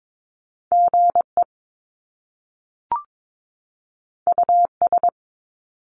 Morse code, heard as a keyed tone, about 22 wpm, 700 Hz. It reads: GE US
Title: Top Two Letter Combinations in Words 22wpm